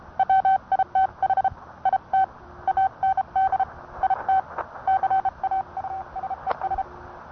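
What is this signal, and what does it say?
Morse code, heard as a keyed tone, about 30 wpm, 770 Hz. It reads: WITH IT AND IT CAUSES
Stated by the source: Morse code sent by N1EA on a Vibroplex key.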